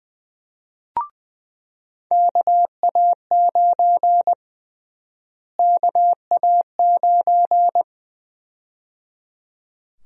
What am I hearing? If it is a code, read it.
Morse code, heard as a keyed tone, about 20 wpm, 700 Hz. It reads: KA9 KA9
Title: USA Call Sign Prefixes - Repeated 2x - 20wpm